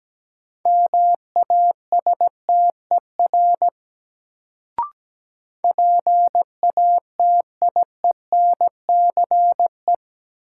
Morse code, heard as a keyed tone, about 17 wpm, 700 Hz. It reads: MASTER PATIENCE
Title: Sentences from Top 1300 Words - New Word Drill Alt - 17wpm